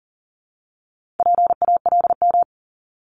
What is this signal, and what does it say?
Morse code, heard as a keyed tone, about 38 wpm, 700 Hz. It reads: PALM
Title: Multi-letter Phonemes 40wpm